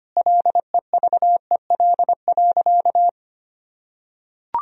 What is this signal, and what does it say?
Morse code, heard as a keyed tone, about 25 wpm, 700 Hz. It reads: LEVEL.
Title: Sentences from Top 1000 Words - Review Part 4 - 25wpm